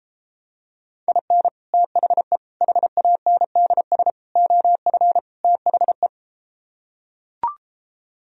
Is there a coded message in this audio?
Yes: Morse code from 1.1 to 6.1 s, heard as a keyed tone, about 33 wpm, 700 Hz. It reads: IN THE HANDS OF THE